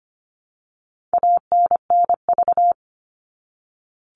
Morse code, heard as a keyed tone, about 25 wpm, 700 Hz. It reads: ANNV
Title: All Cut Numbers - 4-Digit 25wpm